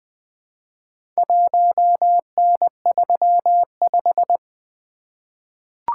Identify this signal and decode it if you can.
Morse code, heard as a keyed tone, about 20 wpm, 700 Hz. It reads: 1N35
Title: Four Letter-Number Combinations - 20wpm